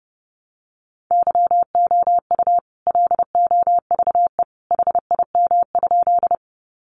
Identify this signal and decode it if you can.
Morse code, heard as a keyed tone, about 30 wpm, 700 Hz. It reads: YOU LOVE HIM?
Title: Sentences from Top 400 Words 30wpm